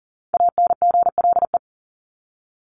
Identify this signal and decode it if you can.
Morse code, heard as a keyed tone, about 40 wpm, 700 Hz. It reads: ANGLE
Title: Top 1251-1500 Words 40wpm